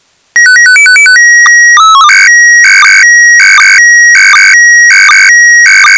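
SSTV (slow-scan television), PD 180.
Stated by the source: SSTV test signal in PD180 mode for checking SSTV software without a radio.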